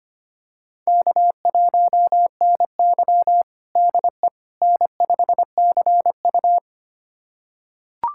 Morse code, heard as a keyed tone, about 25 wpm, 700 Hz. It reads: K1NY DE N5CU
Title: USA Callsign DE Callsign - 4 Characters - 25wpm